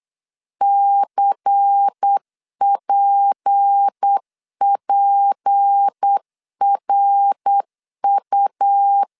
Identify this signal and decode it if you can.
Morse code, heard as a keyed tone, about 8 wpm, 800 Hz. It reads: CPPRU
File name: LEZIONI / 6 PKX0 / 6 P 8wpm 3025